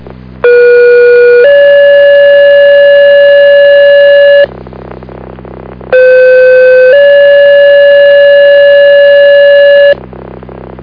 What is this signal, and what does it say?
Two-tone page, heard as a one-second tone, then a three-second tone, about 480 then 570 Hz, about 510 then 570 Hz. It is heard from 0.4 to 9.9 s.